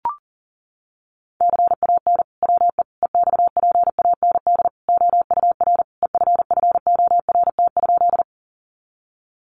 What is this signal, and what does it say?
Morse code, heard as a keyed tone, about 40 wpm, 700 Hz. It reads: CAN WE EXPAND OUR EFFORT?
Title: Sentences from Top 1400 Words 40wpm